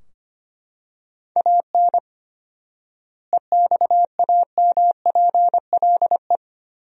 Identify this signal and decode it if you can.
Morse code, heard as a keyed tone, about 25 wpm, 700 Hz. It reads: AN EXAMPLE
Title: 4x Word-Spacing - Sentences from Top 400 Words - 25wpm